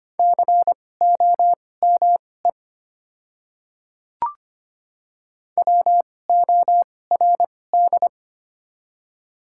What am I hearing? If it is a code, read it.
Morse code, heard as a keyed tone, about 25 wpm, spaced as for 18 wpm, 700 Hz. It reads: COME WORD
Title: 2x Character Spacing - Top 200 Words - 25wpm